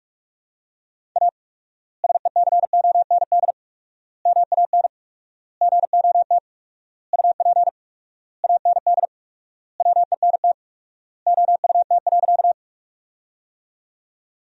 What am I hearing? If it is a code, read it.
Morse code, heard as a keyed tone, about 47 wpm, 700 Hz. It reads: A SECOND MAN GOT UP AND WENT OUT.
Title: 4x Word-Spacing - Sentences from Top 300 Words - 45wpm